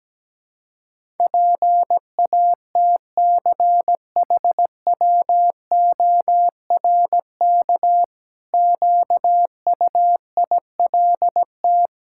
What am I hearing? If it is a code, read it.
Morse code, heard as a keyed tone, about 17 wpm, 700 Hz. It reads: PATCHWORK QUILT
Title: Pangram Challenge v2 - 17wpm